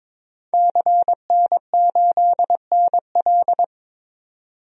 Morse code, heard as a keyed tone, about 22 wpm, 700 Hz. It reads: CN8NL
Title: DXCC-100 Call Signs - 5 Characters - 22wpm